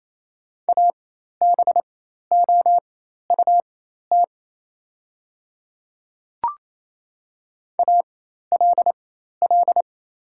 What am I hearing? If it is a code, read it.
Morse code, heard as a keyed tone, about 28 wpm, spaced as for 13 wpm, 690 Hz. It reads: ABOUT ALL
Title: 4x Character Spacing - Top 100 Words - 28wpm